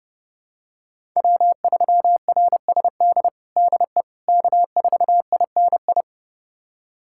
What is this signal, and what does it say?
Morse code, heard as a keyed tone, about 30 wpm, 700 Hz. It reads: W3RSD DE K4INI
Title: USA Callsign DE Callsign - 5 Characters - 30wpm